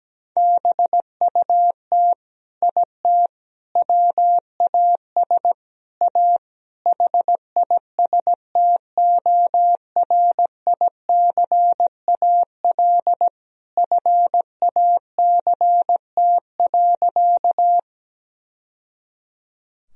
Morse code, heard as a keyed tone, about 17 wpm, 700 Hz. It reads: BUT IT WAS A HISTORICAL FACT.
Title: Sentences from Top 1000 Words - Review Part 4 - 17wpm